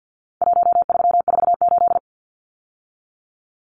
Morse code, heard as a keyed tone, about 50 wpm, 700 Hz. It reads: 1348
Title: Numbers - Contest Style - 4-Digit 50wpm